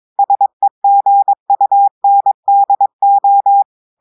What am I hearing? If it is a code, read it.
Morse code, heard as a keyed tone, about 22 wpm, 800 Hz. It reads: SEGUNDO
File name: audios / portugais / portugais texte en clair 1 vitesse 22 mots minutes